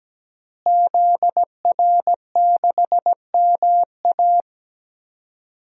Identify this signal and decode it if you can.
Morse code, heard as a keyed tone, about 17 wpm, 700 Hz. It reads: ZR6MA
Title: DXCC-100 Call Signs - 5 Characters - Repeated 3x - 17wpm